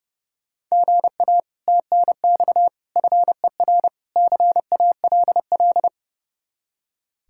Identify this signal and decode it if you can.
Morse code, heard as a keyed tone, about 39 wpm, 700 Hz. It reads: GA TNX FER CALL